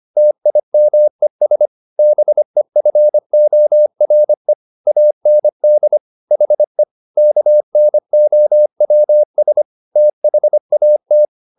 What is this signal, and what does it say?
Morse code, heard as a keyed tone, about 25 wpm, 600 Hz. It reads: TIMES BEFORE AND HE KNOWS THAT
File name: the old man and the sea part 1 25wpm